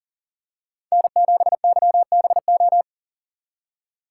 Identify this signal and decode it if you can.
Morse code, heard as a keyed tone, about 39 wpm, 700 Hz. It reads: N7YBO
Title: USA Call Signs - 5 Characters - 3x Character Spacing - 40wpm